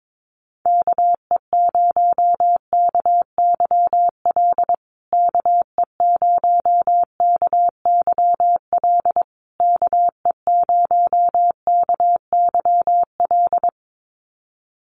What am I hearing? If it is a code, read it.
Morse code, heard as a keyed tone, about 22 wpm, 700 Hz. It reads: KE0KYL KE0KYL KE0KYL
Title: USA Call Signs Repeated 3x 22wpm